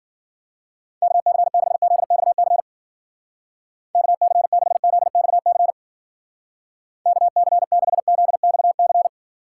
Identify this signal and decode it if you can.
Morse code, heard as a keyed tone, about 51 wpm, 700 Hz. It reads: KC6ZXC KC6ZXC KC6ZXC